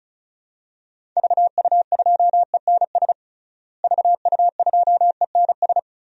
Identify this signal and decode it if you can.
Morse code, heard as a keyed tone, about 34 wpm, 700 Hz. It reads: VU2ENS VU2ENS
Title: DXCC-100 Call Signs - 6 Characters - Repeated 3x - 35wpm